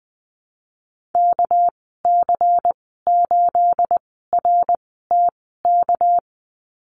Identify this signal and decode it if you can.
Morse code, heard as a keyed tone, about 20 wpm, 700 Hz. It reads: K C 8 R T K
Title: 238 Lesson 38g - Letter q - Call Signs 20wpm